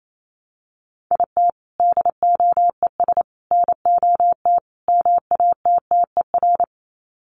Morse code, heard as a keyed tone, about 28 wpm, 700 Hz. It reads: IT DOES NOT MATTER
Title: Sets of 4 Words 28wpm